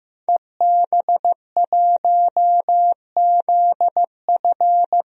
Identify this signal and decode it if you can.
Morse code, heard as a keyed tone, about 15 wpm, 700 Hz. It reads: EB1ZF
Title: DXCC-100 Call Signs - 5 Characters - 15wpm